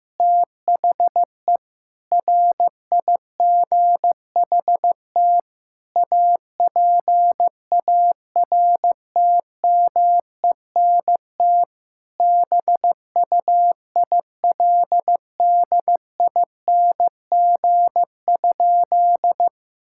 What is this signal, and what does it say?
Morse code, heard as a keyed tone, about 15 wpm, 700 Hz. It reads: THE RIGHT APARTMENT BUILDING?
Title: Sentences from Top 1000 Words - Review Part 6 - 15wpm